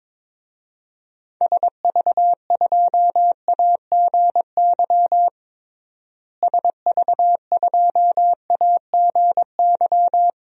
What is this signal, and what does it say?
Morse code, heard as a keyed tone, about 22 wpm, 700 Hz. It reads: SV2AGY SV2AGY
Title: DXCC-100 Call Signs - 6 Characters - Repeated 2x - 22wpm